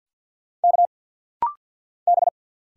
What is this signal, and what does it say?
Morse code, heard as a keyed tone, about 50 wpm, 700 Hz. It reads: K B
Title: Single Letter-Number Alt - Rapid-Fire 50wpm